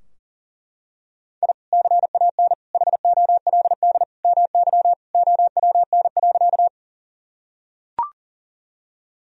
Morse code, heard as a keyed tone, about 37 wpm, 700 Hz. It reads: I CAN HOLD MY OWN.
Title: Sentences from Top 1000 Words - Review Part 2 - 40wpm